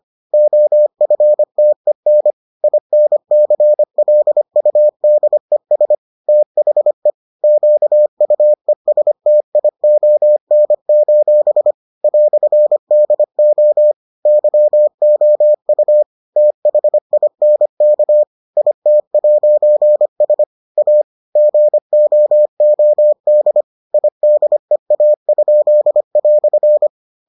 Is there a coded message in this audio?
Yes: Morse code from 0.3 to 26.9 s, heard as a keyed tone, about 25 wpm, 600 Hz. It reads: OFTEN INCLUDES THE QUESTION: "DO YOU THINK IT'S A GOOD IDEA?"